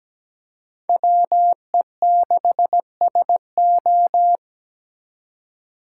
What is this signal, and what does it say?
Morse code, heard as a keyed tone, about 20 wpm, 700 Hz. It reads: WE6SO
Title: Speed Racing - USA Call Signs 17wpm